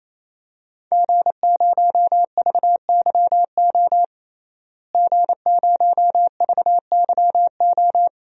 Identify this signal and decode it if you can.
Morse code, heard as a keyed tone, about 27 wpm, 700 Hz. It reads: G0VYO G0VYO
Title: DXCC-100 Call Signs - 5 Characters - Repeated 3x - 28wpm